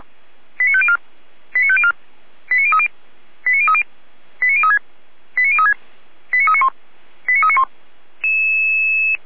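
Five-tone selective call (ZVEI), heard as quick runs of five tones, tones of about 70 ms. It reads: 89504 89504 89030 89030 89036 89036 89391 89391
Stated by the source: ZVEI five-tone selective call received with an SDR.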